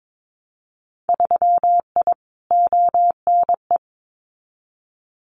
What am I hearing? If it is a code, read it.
Morse code, heard as a keyed tone, about 22 wpm, 700 Hz. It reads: 3I ONE
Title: Winter Field Day Exchanges Encore 22wpm